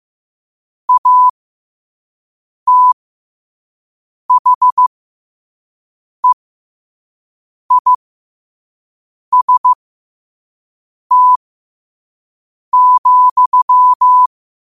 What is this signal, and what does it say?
Morse code, heard as a keyed tone, about 15 wpm, spaced as for 5 wpm, 1000 Hz. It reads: ATHEIST,
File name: qotdfiles / 5wpm